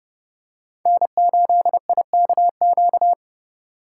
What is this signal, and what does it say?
Morse code, heard as a keyed tone, about 30 wpm, 700 Hz. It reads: N8IKQ